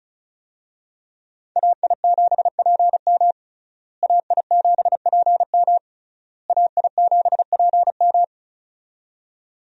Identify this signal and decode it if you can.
Morse code, heard as a keyed tone, about 36 wpm, 700 Hz. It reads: AI7PM AI7PM AI7PM
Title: USA Call Signs - 5 Characters - Repeated 3x - 35wpm